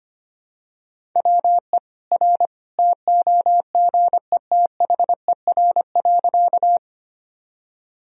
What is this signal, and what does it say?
Morse code, heard as a keyed tone, about 25 wpm, 700 Hz. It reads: WE R TOGETHER.
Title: Sentences from Top 300 Words and Abbreviations 25wpm